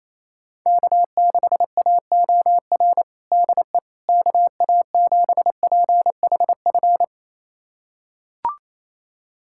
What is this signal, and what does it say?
Morse code, heard as a keyed tone, about 28 wpm, 700 Hz. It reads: K6AOR DE KA7PHF
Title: USA Callsign DE Callsign 28wpm